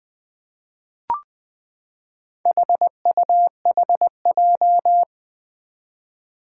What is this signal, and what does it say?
Morse code, heard as a keyed tone, about 20 wpm, 700 Hz. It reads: HUHJ
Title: Four Challenging Letter-Number Combinations 20wpm